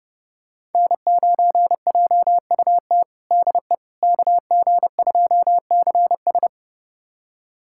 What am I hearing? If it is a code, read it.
Morse code, heard as a keyed tone, about 30 wpm, 700 Hz. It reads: N9JUT DE KG2CS
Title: USA Callsign DE Callsign - 5 Characters - 30wpm